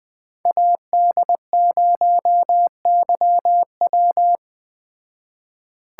Morse code, heard as a keyed tone, about 20 wpm, 700 Hz. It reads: AD0YW
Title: Speed Racing - USA 2x2 Call Signs 17wpm